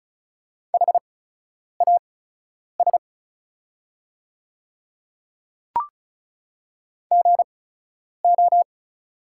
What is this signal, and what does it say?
Morse code, heard as a keyed tone, about 36 wpm, spaced as for 10 wpm, 700 Hz. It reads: HAS GO
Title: 8x Character Spacing - Top 100 Words - 35wpm